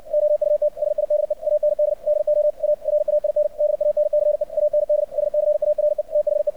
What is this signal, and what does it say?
Morse code, heard as a keyed tone, about 59 wpm, 600 Hz. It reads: OUT SIDE AND DO A RAIN DANCE AND SCARE THE
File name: archived / CRSnet